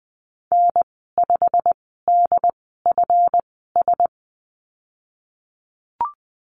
Morse code, heard as a keyed tone, about 20 wpm, spaced as for 15 wpm, 700 Hz. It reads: N5DFS
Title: 173 Lesson 28g - Number 9 - Call Signs 20wpm